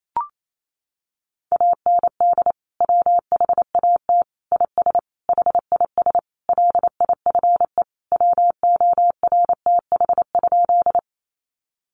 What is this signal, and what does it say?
Morse code, heard as a keyed tone, about 28 wpm, 700 Hz. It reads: AND WHAT IS HIS LIFE WORTH?
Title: Sentences from Top 800 Words 28wpm